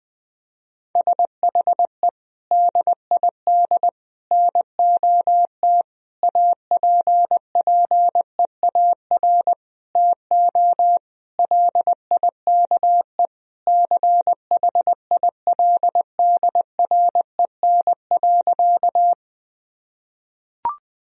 Morse code, heard as a keyed tone, about 21 wpm, 700 Hz. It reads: SHE DID NOT APPEAR TO LIKE CHILDREN.
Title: Sentences from Top 1000 Words - Review Part 2 - 20wpm